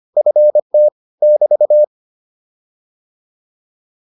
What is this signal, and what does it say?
Morse code, heard as a keyed tone, about 25 wpm, 600 Hz. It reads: FT <BT>
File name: QSO304a 25